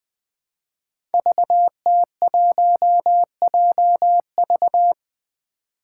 Morse code, heard as a keyed tone, about 20 wpm, 700 Hz. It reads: VT1JV
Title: Five Challenging Letter-Number Combinations 20wpm